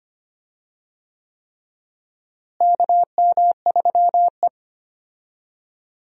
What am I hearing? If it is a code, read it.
Morse code, heard as a keyed tone, about 25 wpm, 700 Hz. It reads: KM3E
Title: USA Call Signs - 4 Characters - Repeated 3x - 25wpm